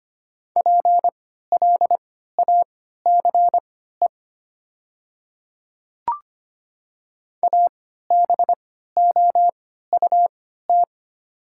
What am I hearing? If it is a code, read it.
Morse code, heard as a keyed tone, about 25 wpm, spaced as for 14 wpm, 690 Hz. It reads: PLACE ABOUT
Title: 3x Character Spacing - Top 200 Words - 25wpm